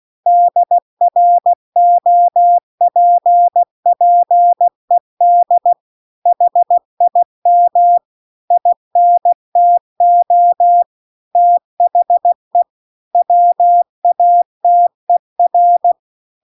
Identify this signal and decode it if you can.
Morse code, heard as a keyed tone, about 16 wpm, 700 Hz. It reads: DROPPED HIM INTO THE WATER
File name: CW-The-Big-Twoheaded-River-2-16WPM